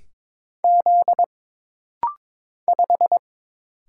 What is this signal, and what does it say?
Morse code, heard as a keyed tone, about 22 wpm, 700 Hz. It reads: Z 5
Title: Single Letter-Number Alt - Rapid-Fire 22wpm